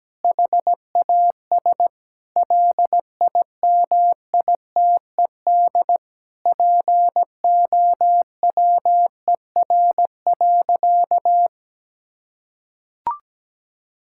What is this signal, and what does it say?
Morse code, heard as a keyed tone, about 17 wpm, 700 Hz. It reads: HAS LIMITED POWER.
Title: Sentences from Top 1000 Words - Review Part 6 - 17wpm